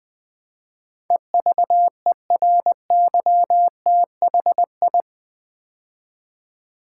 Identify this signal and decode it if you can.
Morse code, heard as a keyed tone, about 20 wpm, 700 Hz. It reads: EVERYTHI
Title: Build A Word - 4 Letters At A Time - 10-Letter Words - 20wpm